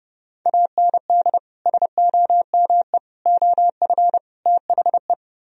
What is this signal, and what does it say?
Morse code, heard as a keyed tone, about 30 wpm, 700 Hz. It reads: AND SOME OF THE